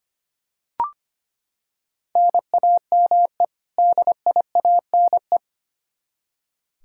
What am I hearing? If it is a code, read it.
Morse code, heard as a keyed tone, about 25 wpm, 700 Hz. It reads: NAME DIANE
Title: K1USN SST QSO Practice 25wpm